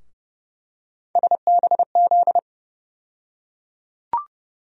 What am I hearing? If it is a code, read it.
Morse code, heard as a keyed tone, about 30 wpm, 700 Hz. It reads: SBZ
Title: USA Call Sign Postfixes - 3 Letters 30wpm